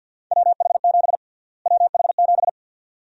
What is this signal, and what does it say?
Morse code, heard as a keyed tone, about 51 wpm, 700 Hz. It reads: WH7 WH7
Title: USA Call Sign Prefixes - Repeated 2x - 50wpm